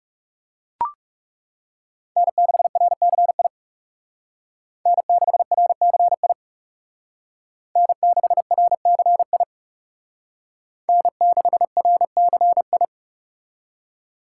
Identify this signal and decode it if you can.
Morse code, heard as a keyed tone, about 38 wpm, 700 Hz. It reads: N6RCI N6RCI N6RCI N6RCI